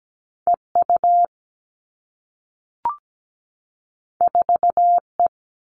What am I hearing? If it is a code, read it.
Morse code, heard as a keyed tone, about 17 wpm, 700 Hz. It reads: EU 4E
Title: All Cut Numbers - 2-Digit 17wpm